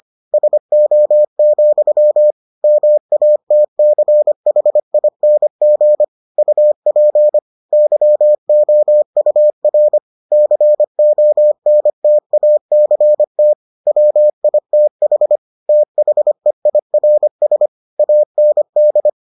Morse code, heard as a keyed tone, about 25 wpm, 600 Hz. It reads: SO, MATCHING UP YOUR CONTACT WITH THEIRS AND